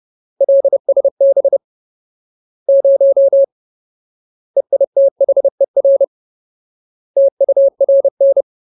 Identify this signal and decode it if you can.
Morse code, heard as a keyed tone, about 30 wpm, 550 Hz. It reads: LSB 0 EITHER TURN
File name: Ultimate Word List / 30WPM Ultimate